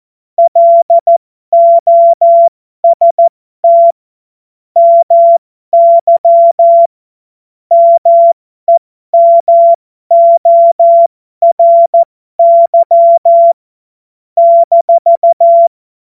Morse code, heard as a keyed tone, about 14 wpm, spaced as for 12 wpm, 680 Hz. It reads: LOST MY MEMORY -
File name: Morsekurs / Witze